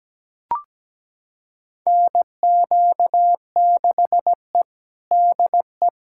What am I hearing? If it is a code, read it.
Morse code, heard as a keyed tone, about 17 wpm, 700 Hz. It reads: NQ6E DE
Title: USA Callsign DE Callsign - 4 Characters - 17wpm